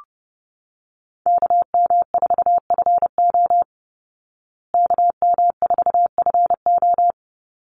Morse code, heard as a keyed tone, about 30 wpm, 700 Hz. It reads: KM4FO KM4FO
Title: Call Signs Repeated 3x - 30wpm